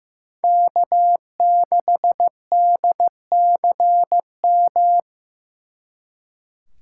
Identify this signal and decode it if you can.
Morse code, heard as a keyed tone, about 18 wpm, 700 Hz. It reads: K6DCM
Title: Speed Racing - USA 1x3 Call Signs 15wpm